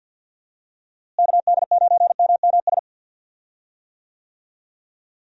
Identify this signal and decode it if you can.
Morse code, heard as a keyed tone, about 50 wpm, 700 Hz. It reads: KD9MMS
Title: USA Call Signs - 6 Characters - Repeated 3x - 50wpm